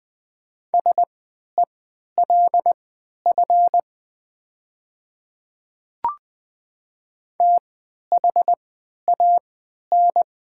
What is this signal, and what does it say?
Morse code, heard as a keyed tone, about 20 wpm, spaced as for 11 wpm, 700 Hz. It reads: SELF THAN
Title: 3x Character Spacing - Top 200 Words - 20wpm